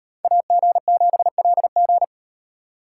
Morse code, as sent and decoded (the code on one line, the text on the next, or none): .- --. --... .-.. --.
AG7LG